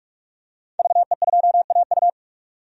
Morse code, heard as a keyed tone, about 44 wpm, 700 Hz. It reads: VE2AU
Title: DXCC-100 Call Signs - 5 Characters - 45wpm